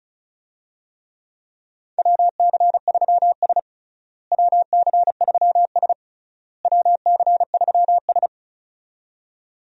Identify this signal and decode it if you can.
Morse code, heard as a keyed tone, about 34 wpm, 700 Hz. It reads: WC3S WC3S WC3S
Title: USA Call Signs - 4 Characters - Repeated 3x - 35wpm